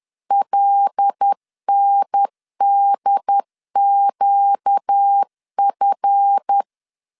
Morse code, heard as a keyed tone, about 10 wpm, 800 Hz. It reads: LNDQF